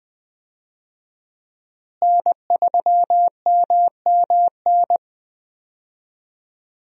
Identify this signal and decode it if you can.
Morse code, heard as a keyed tone, about 20 wpm, 700 Hz. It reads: N3MMN
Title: USA Call Signs - 5 Characters - Repeated 3x - 20wpm